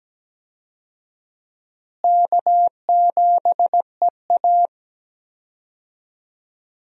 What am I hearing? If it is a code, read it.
Morse code, heard as a keyed tone, about 17 wpm, 700 Hz. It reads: K7EA
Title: USA Call Signs - 4 Characters - Repeated 3x - 17wpm